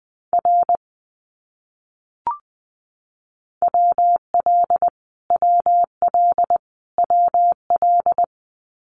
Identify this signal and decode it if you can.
Morse code, heard as a keyed tone, about 20 wpm, 700 Hz. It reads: R WL WL WL
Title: Abbreviations Repeated 3x - 20wpm